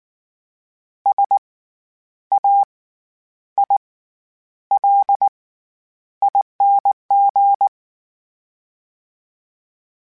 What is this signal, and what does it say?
Morse code, heard as a keyed tone, about 19 wpm, spaced as for 9 wpm, 800 Hz. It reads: SAILING
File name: Special INGs / Special INGs-mp3s / SlowToFaster ing @18